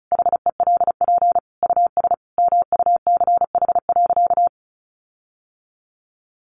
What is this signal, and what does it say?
Morse code, heard as a keyed tone, about 34 wpm, 700 Hz. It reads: HELP US MUCH.
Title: Sentences from Top 1500 Words 35wpm